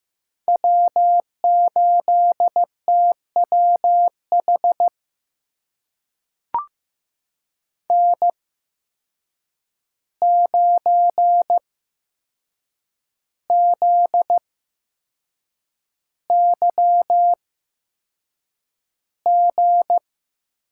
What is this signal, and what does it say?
Morse code, heard as a keyed tone, about 15 wpm, spaced as for 6 wpm, 700 Hz. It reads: W8TWH N9ZYG
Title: USA Call Signs - 5 Characters - 8x Character Spacing - 15wpm